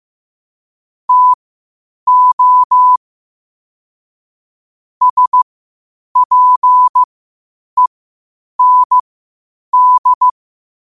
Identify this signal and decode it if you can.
Morse code, heard as a keyed tone, about 15 wpm, spaced as for 8 wpm, 1000 Hz. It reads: TO SPEND
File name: qotdfiles / 8wpm